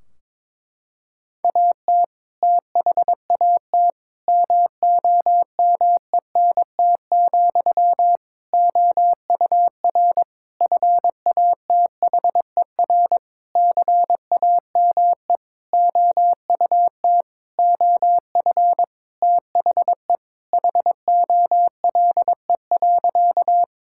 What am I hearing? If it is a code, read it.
Morse code, heard as a keyed tone, about 23 wpm, 700 Hz. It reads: AT THAT MOMENT, OUR FATHER CAME OUT OF THE HOLE.
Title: Sentences from Top 1600 Words 22wpm